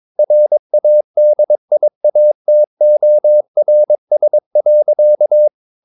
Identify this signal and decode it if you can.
Morse code, heard as a keyed tone, about 22 wpm, 600 Hz. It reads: RADIATORS.
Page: Romanen Never af Ken Follet på engelsk.